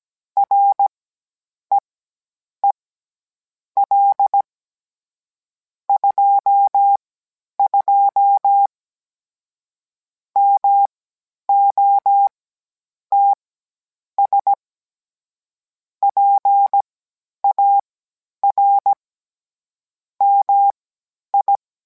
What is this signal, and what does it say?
Morse code, heard as a keyed tone, about 17 wpm, spaced as for 9 wpm, 800 Hz. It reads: REEL 22 MOTS PAR MI